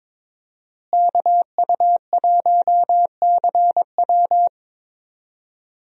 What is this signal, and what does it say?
Morse code, heard as a keyed tone, about 21 wpm, 700 Hz. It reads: KU1CW
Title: USA Call Signs - 5 Characters - Repeated 3x - 22wpm